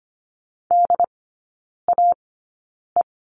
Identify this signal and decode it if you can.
Morse code, heard as a keyed tone, about 25 wpm, spaced as for 9 wpm, 700 Hz. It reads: DAE
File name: Lesson 17e - Letter u - Three Characters 25wpm